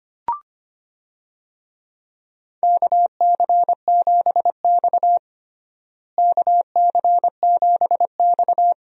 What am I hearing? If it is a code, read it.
Morse code, heard as a keyed tone, about 25 wpm, 700 Hz. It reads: KC7X KC7X
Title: USA Call Signs - 4 Characters - Repeated 3x - 25wpm